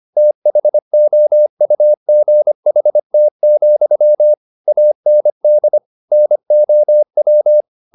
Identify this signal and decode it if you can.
Morse code, heard as a keyed tone, about 25 wpm, 600 Hz. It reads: THOUGHT, AND NOW